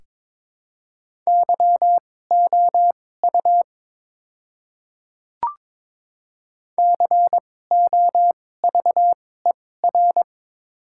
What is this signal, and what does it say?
Morse code, heard as a keyed tone, about 22 wpm, spaced as for 16 wpm, 700 Hz. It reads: YOU COVER
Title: 2x Character Spacing - Top 300 Words - 22wpm